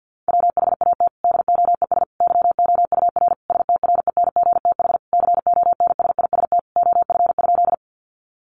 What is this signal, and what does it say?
Morse code, heard as a keyed tone, about 50 wpm, 700 Hz. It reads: WHAT DOES YOUR STRENGTH CONSIST OF?
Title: Sentences from Top 1400 Words 50wpm